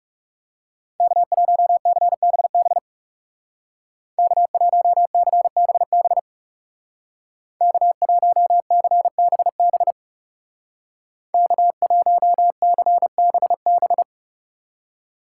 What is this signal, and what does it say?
Morse code, heard as a keyed tone, about 39 wpm, 700 Hz. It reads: K1CBB K1CBB K1CBB K1CBB